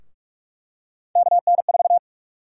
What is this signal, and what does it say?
Morse code, heard as a keyed tone, about 46 wpm, 700 Hz. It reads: KN4
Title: USA Call Sign Prefixes - Repeated 2x - 45wpm